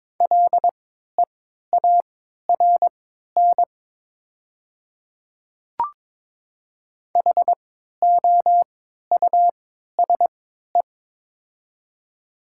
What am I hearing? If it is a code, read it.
Morse code, heard as a keyed tone, about 22 wpm, spaced as for 13 wpm, 700 Hz. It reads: LEARN HOUSE
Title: 3x Character Spacing - Top 500 Words - 22wpm